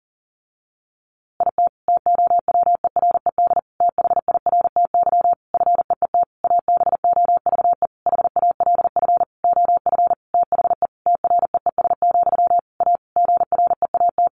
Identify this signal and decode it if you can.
Morse code, heard as a keyed tone, about 40 wpm, 700 Hz. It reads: IT TOWERED THIRTY FEET ABOVE HALF OF THE TREES, A GREAT
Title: A Sound Of Thunder 40wpm